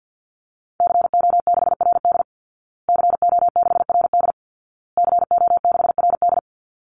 Morse code, heard as a keyed tone, about 49 wpm, 700 Hz. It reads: CO6RD CO6RD CO6RD
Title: Call Signs Repeated 2x - 50wpm